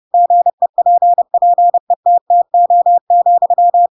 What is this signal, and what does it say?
Morse code, heard as a keyed tone, about 30 wpm, 700 Hz. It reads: GEPPETTO,
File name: Pinocchio / Pinocchio ch7 30 30 ik1hge